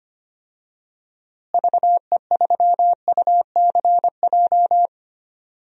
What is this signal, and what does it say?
Morse code, heard as a keyed tone, about 25 wpm, 700 Hz. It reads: VE3UCJ
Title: DXCC-100 Call Signs - 6 Characters - Repeated 3x - 25wpm